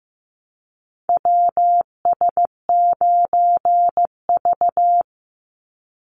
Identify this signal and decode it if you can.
Morse code, heard as a keyed tone, about 15 wpm, 700 Hz. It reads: WS9V
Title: Call Signs Repeated 2x - 15wpm